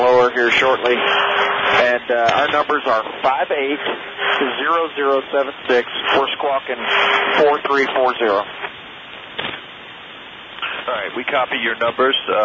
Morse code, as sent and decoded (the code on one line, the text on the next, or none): none